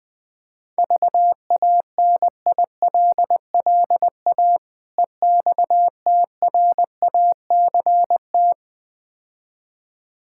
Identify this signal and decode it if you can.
Morse code, heard as a keyed tone, about 21 wpm, 700 Hz. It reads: VANILLA EXTRACT
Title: Difficult Words - Speed-Racing with Hints - 20wpm